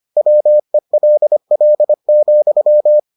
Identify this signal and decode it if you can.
Morse code, heard as a keyed tone, about 25 wpm, 600 Hz. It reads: WELL,